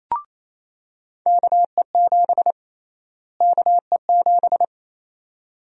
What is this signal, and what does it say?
Morse code, heard as a keyed tone, about 27 wpm, 700 Hz. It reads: KE7 KE7
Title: USA Call Sign Prefixes - Repeated 2x - 28wpm